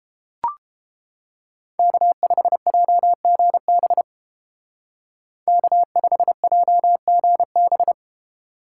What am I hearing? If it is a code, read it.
Morse code, heard as a keyed tone, about 28 wpm, 700 Hz. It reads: K5JGB K5JGB